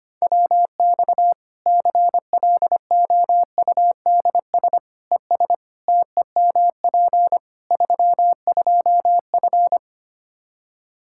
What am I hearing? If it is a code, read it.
Morse code, heard as a keyed tone, about 25 wpm, 700 Hz. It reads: WX CLOUDS ES TEMP 32F
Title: Practice QSO - Exchange 2 - Info - 25wpm